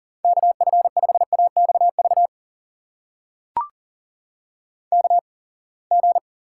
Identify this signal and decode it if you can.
Morse code, heard as a keyed tone, about 41 wpm, spaced as for 28 wpm, 700 Hz. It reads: KF5AXV KG
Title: USA Call Signs - 6 Characters - 8x Character Spacing - 40wpm